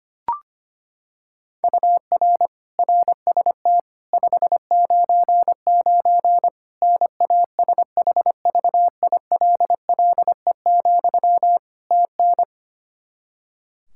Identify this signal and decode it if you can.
Morse code, heard as a keyed tone, about 25 wpm, 700 Hz. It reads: UR RST 599 NASHVILLE, TN
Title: Practice QSO - Exchange 1 - Beginning - 25wpm